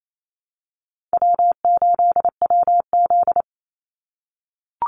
Morse code, heard as a keyed tone, about 28 wpm, 700 Hz. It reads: W8WZ
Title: Call Signs - 28wpm